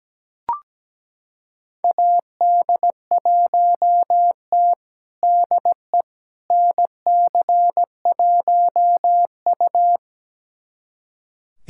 Morse code, heard as a keyed tone, about 17 wpm, 700 Hz. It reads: AD1T DE NC1U
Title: USA Callsign DE Callsign - 4 Characters - 17wpm